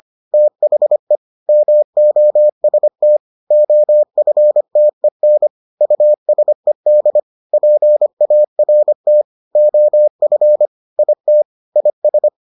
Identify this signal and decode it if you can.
Morse code, heard as a keyed tone, about 25 wpm, 600 Hz. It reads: THE MOST OFTEN USED PART OF IT IS